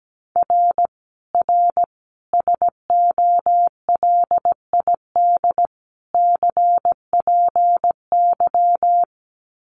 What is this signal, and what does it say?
Morse code, heard as a keyed tone, about 17 wpm, 700 Hz. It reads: R R SOLID CPY
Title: Common QSO Phrases 17wpm